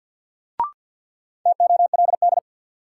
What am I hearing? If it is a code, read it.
Morse code, heard as a keyed tone, about 46 wpm, 700 Hz. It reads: TOLD
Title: Sentences from Top 400 Words - New Word Drill - 50wpm